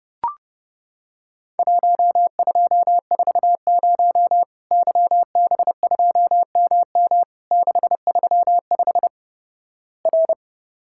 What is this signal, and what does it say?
Morse code, heard as a keyed tone, about 30 wpm, 700 Hz. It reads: 1240 YB2MM 635 R
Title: Worked All Europe DX Contest - QTC Exchange - 30wpm